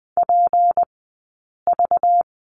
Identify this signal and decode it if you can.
Morse code, heard as a keyed tone, about 20 wpm, 700 Hz. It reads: P V
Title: 190 Lesson 31d - Letter v - Two Characters 20wpm